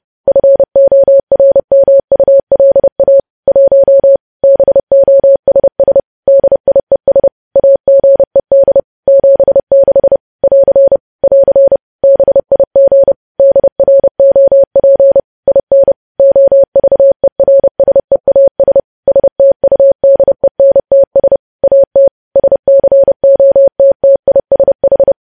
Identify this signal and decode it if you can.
Morse code, heard as a keyed tone, about 30 wpm, 550 Hz. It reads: FORMULA 1 BOSS DIES AGED 76 <AR> <AR> BIG DROP IN OVERSEAS STUDENTS AT SCOTTISH
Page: news headlines in morse code at 30 wpm